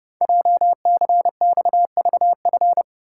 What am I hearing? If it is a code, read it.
Morse code, heard as a keyed tone, about 30 wpm, 700 Hz. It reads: JCXVF
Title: Code Groups - 5 Letters - 30wpm